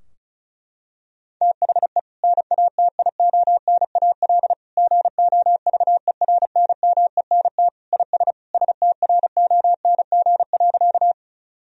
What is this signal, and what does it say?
Morse code, heard as a keyed tone, about 36 wpm, 700 Hz. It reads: THE NATIONAL GOVERNMENT IS STRONG.
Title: Sentences from Top 1000 Words - Review Part 4 - 35wpm